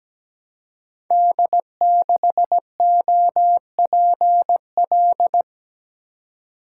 Morse code, heard as a keyed tone, about 17 wpm, 700 Hz. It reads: D6OPL
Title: Code Groups - 5 Letter:Numbers - 17wpm